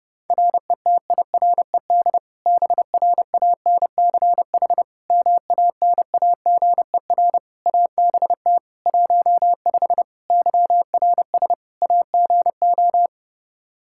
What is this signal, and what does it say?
Morse code, heard as a keyed tone, about 32 wpm, 700 Hz. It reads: RETIRED BRANCH MANAGER ABT 15 YRS AGO
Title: Practice QSO - Exchange 3 - More Info - 30wpm